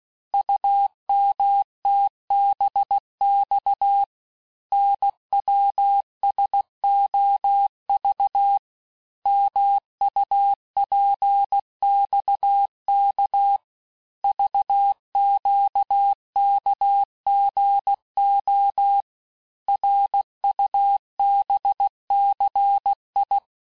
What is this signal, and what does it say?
Morse code, heard as a keyed tone, about 16 wpm, 800 Hz. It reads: UMTBX NWSOV MUPXK VQKGO RUBCI
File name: cours-du-lundi files / Cours-1586-texte-chiffre-vitesse-15mpm